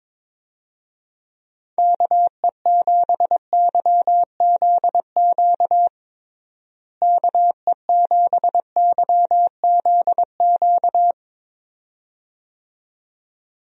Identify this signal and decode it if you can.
Morse code, heard as a keyed tone, about 22 wpm, 700 Hz. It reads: KE7YZQ KE7YZQ
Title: USA Call Signs - 6 Characters - Repeated 2x - 22wpm